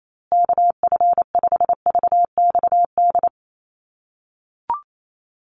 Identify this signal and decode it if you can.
Morse code, heard as a keyed tone, about 28 wpm, 700 Hz. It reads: KF5VXD
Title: USA Call Signs Repeated 3x 28wpm